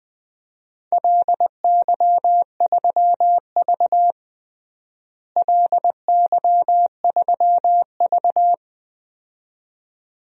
Morse code, heard as a keyed tone, about 20 wpm, 700 Hz. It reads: LY3V LY3V
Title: DXCC-100 Call Signs - 4 Characters - Repeated 2x - 20wpm